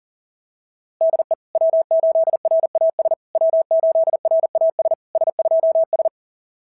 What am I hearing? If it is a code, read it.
Morse code, heard as a keyed tone, about 40 wpm, 690 Hz. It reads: DE W8RAS W8RAS S2S
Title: SOTA QSO Practice 40wpm